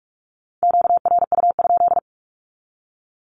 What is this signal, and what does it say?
Morse code, heard as a keyed tone, about 46 wpm, 700 Hz. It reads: QRU?
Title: Q-Codes with Spoken Meaning - 45wpm